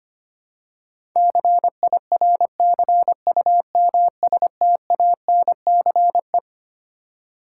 Morse code, heard as a keyed tone, about 25 wpm, 700 Hz. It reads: CIRCUMSTANCE
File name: Twelve Letter Words - 25wpm